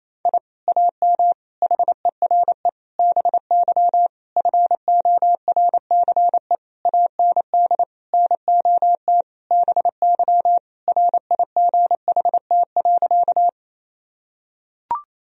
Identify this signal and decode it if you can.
Morse code, heard as a keyed tone, about 28 wpm, 700 Hz. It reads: I AM HERE BY FORCE AND NOT BY RIGHT.
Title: Sentences from Top 1000 Words - Review Part 3 - 28wpm